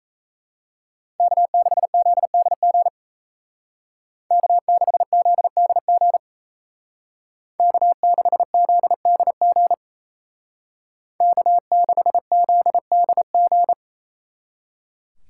Morse code, heard as a keyed tone, about 36 wpm, 700 Hz. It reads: K6ZDG K6ZDG K6ZDG K6ZDG